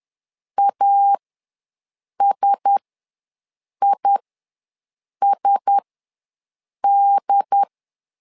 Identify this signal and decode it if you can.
Morse code, heard as a keyed tone, about 10 wpm, spaced as for 6 wpm, 800 Hz. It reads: ASISD
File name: LEZIONI / 4 Q1S ws 10wpm 2208